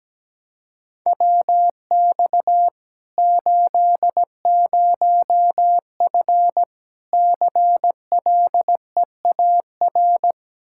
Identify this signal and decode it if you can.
Morse code, heard as a keyed tone, about 17 wpm, 700 Hz. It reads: WX 80F CLEAR
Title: Practice QSO - Exchange 2 - Info - 17wpm